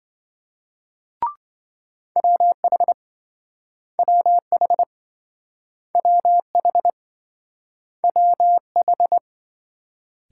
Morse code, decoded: WH WH WH WH

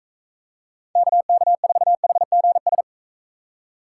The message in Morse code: KK4HGS

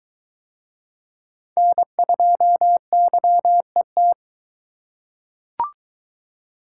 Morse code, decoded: N2YET

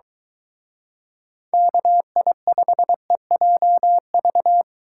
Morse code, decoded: KI5EJV

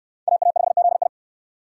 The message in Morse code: AA5ZI

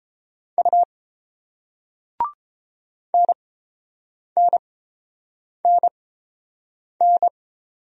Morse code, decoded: U NNNN

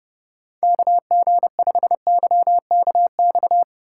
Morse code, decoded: KG5YKX

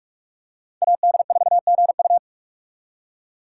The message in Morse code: AD4GU

